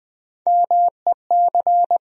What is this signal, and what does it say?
Morse code, heard as a keyed tone, about 20 wpm, 700 Hz. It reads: MEC
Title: USA Call Sign Postfixes - 3 Letters 20wpm